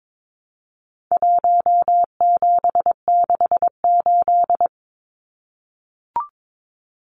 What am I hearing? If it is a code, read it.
Morse code, heard as a keyed tone, about 22 wpm, 700 Hz. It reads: 1768
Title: Numbers - Contest Style - 4-Digit 22wpm